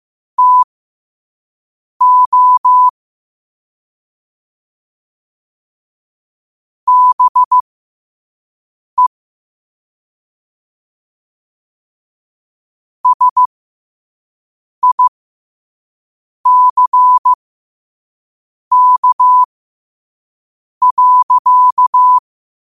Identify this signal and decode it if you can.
Morse code, heard as a keyed tone, about 15 wpm, spaced as for 5 wpm, 1000 Hz. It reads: TO BE SICK.
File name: qotdfiles / 5wpm